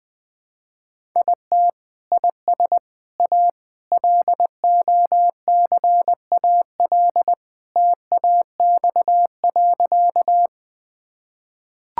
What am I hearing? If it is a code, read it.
Morse code, heard as a keyed tone, about 21 wpm, 700 Hz. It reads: IT IS A LOCAL TAX.
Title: Sentences from Top 1000 Words - Review Part 5 - 20wpm